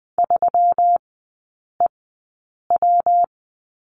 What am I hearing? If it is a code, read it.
Morse code, heard as a keyed tone, about 20 wpm, spaced as for 8 wpm, 700 Hz. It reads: 3EW
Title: 121 Lesson 20e - Number 3 - Three Characters 20wpm